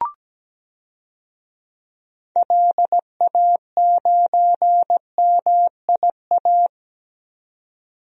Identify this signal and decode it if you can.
Morse code, heard as a keyed tone, about 17 wpm, 700 Hz. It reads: LA9MIA
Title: DXCC-100 Call Signs - 6 Characters - 17wpm